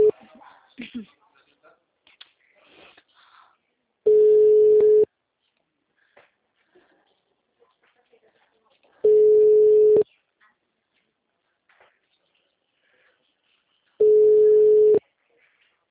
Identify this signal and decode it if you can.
Ringback tone (Europe, 425 Hz), with speech.